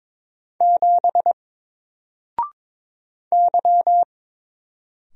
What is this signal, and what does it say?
Morse code, heard as a keyed tone, about 22 wpm, 700 Hz. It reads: 7 Y
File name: Single-Letter-Number-22wpm